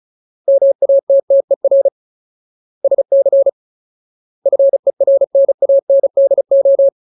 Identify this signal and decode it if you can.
Morse code, heard as a keyed tone, about 35 wpm, 550 Hz. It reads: MATTER SC FERNANDO
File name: Ultimate Word List / 35WPM Ultimate